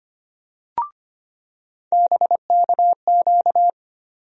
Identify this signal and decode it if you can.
Morse code, heard as a keyed tone, about 25 wpm, 700 Hz. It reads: BKQ
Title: USA Call Sign Postfixes - 3 Letters 25wpm